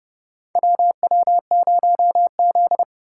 Morse code, heard as a keyed tone, about 30 wpm, 700 Hz. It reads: WW0Z
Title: USA Call Signs - 4 Characters - 2x Character Spacing - 30wpm